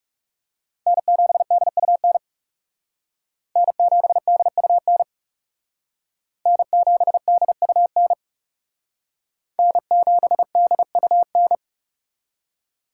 Morse code, heard as a keyed tone, about 36 wpm, 700 Hz. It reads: N7DUN N7DUN N7DUN N7DUN
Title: Speed Racing - USA 1x3 Call Signs 30wpm